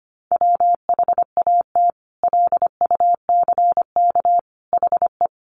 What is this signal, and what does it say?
Morse code, heard as a keyed tone, about 25 wpm, 700 Hz. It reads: WHAT LUCK HE
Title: Sentences from Top 1500 Words 25wpm